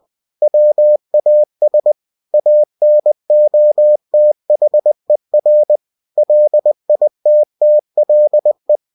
Morse code, heard as a keyed tone, about 20 wpm, 600 Hz. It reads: WAS ANOTHER LITTLE
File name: cw-books / winnie-the-pooh / winnie-20wpm-page-0007